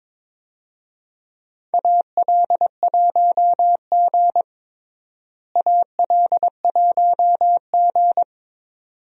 Morse code, heard as a keyed tone, about 21 wpm, 700 Hz. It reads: AL1G AL1G
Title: USA Call Signs - 4 Characters - Repeated 2x - 22wpm